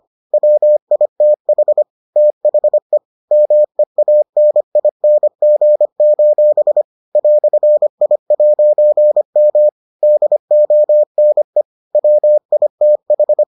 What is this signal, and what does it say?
Morse code, heard as a keyed tone, about 25 wpm, 600 Hz. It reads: WITH THE MEANING: "I'M DONE WITH